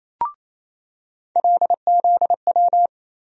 Morse code, heard as a keyed tone, about 27 wpm, 700 Hz. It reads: LZW
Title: USA Call Sign Postfixes - 3 Letters 28wpm